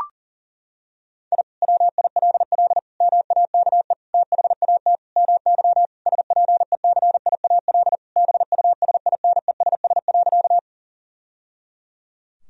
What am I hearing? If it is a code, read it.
Morse code, heard as a keyed tone, about 37 wpm, 700 Hz. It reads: I WILL MAKE THAT MY SPECIAL BUSINESS.
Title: Sentences from Top 1000 Words - Review Part 3 - 40wpm